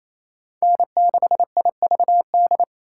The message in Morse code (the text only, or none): N6IVD